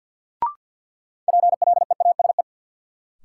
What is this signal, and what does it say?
Morse code, heard as a keyed tone, about 53 wpm, 700 Hz. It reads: PLEASE
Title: Sentences from Top 1500 Words - New Word Drill - 50wpm